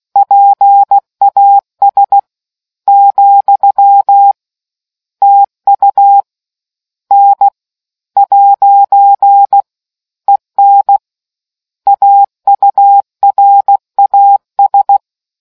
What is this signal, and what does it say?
Morse code, heard as a keyed tone, about 16 wpm, spaced as for 13 wpm, 800 Hz. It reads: PAS, TU N'EN AURAS